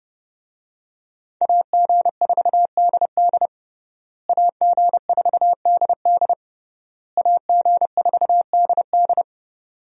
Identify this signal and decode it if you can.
Morse code, heard as a keyed tone, about 30 wpm, 700 Hz. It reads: AG4DD AG4DD AG4DD
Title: USA Call Signs - 5 Characters - Repeated 3x - 30wpm